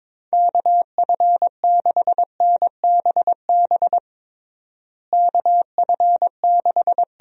Morse code, heard as a keyed tone, about 22 wpm, 700 Hz. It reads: KF6NBB KF6
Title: USA Call Signs - 6 Characters - Repeated 3x - 22wpm